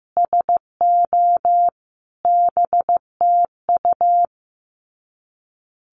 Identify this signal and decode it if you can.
Morse code, heard as a keyed tone, about 15 wpm, 700 Hz. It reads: SO BTU
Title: Common QSO Phrases 15wpm